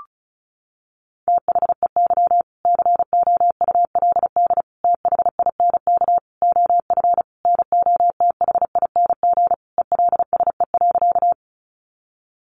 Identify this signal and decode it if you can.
Morse code, heard as a keyed tone, about 34 wpm, 700 Hz. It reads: THEY COULD THINK OF NOTHING ELSE.
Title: Sentences from Top 800 Words 35wpm